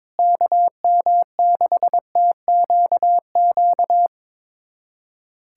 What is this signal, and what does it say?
Morse code, heard as a keyed tone, about 22 wpm, 700 Hz. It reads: KM6TQQ
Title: USA Callsign DE Callsign 22wpm